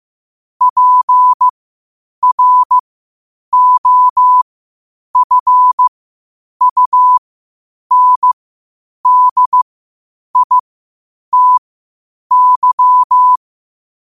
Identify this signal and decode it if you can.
Morse code, heard as a keyed tone, about 15 wpm, spaced as for 8 wpm, 1000 Hz. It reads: PROFUNDITY
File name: qotdfiles / 8wpm